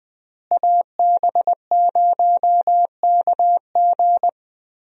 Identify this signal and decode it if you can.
Morse code, heard as a keyed tone, about 21 wpm, 700 Hz. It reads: AB0KG
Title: USA Call Signs - 5 Characters - Repeated 3x - 20wpm